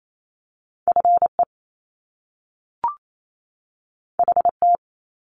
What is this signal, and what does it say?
Morse code, heard as a keyed tone, about 28 wpm, 700 Hz. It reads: FE HT
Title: Top Two Letter Combinations in Words 28wpm